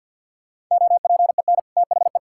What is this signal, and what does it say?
Morse code, heard as a keyed tone, about 50 wpm, 700 Hz. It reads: OPEN THE
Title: Sentences from Top 1600 Words 50wpm